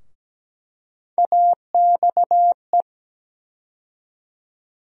Morse code, heard as a keyed tone, about 17 wpm, 700 Hz. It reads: AXE
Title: USA Call Sign Postfixes - 3 Letters 17wpm